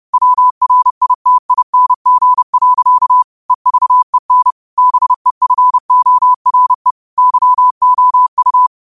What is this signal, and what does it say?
Morse code, heard as a keyed tone, about 30 wpm, 1000 Hz. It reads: WRITING. EVEN BEFORE YOU
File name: qotdfiles / 30wpm